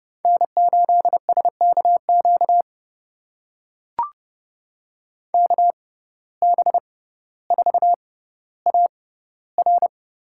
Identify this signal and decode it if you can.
Morse code, heard as a keyed tone, about 30 wpm, spaced as for 15 wpm, 700 Hz. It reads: N8SKQ KB4AR
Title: USA Call Signs - 6x Character Spacing - 30wpm